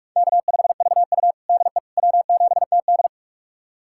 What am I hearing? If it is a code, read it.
Morse code, heard as a keyed tone, about 44 wpm, 690 Hz. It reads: K5VU DE W7TD
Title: USA Callsign DE Callsign - 4 Characters - 45wpm